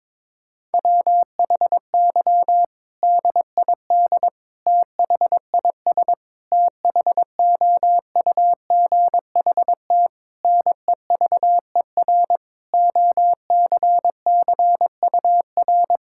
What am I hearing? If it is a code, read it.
Morse code, heard as a keyed tone, about 22 wpm, 700 Hz. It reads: WHY DID THIS THOUGHT NEVER OCCUR
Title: Sentences from Top 1000 Words - Review Part 6 - 22wpm